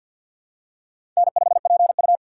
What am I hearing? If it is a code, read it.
Morse code, heard as a keyed tone, about 53 wpm, 700 Hz. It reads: N5PU
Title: K1USN SST QSO Practice - 50wpm